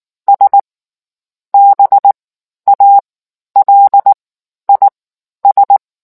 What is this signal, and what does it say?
Morse code, heard as a keyed tone, about 19 wpm, spaced as for 11 wpm, 800 Hz. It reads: S BALIS